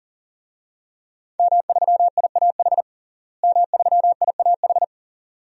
Morse code, heard as a keyed tone, about 40 wpm, 700 Hz. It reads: M3IAH M3IAH
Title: DXCC-100 Call Signs - 5 Characters - Repeated 2x - 40wpm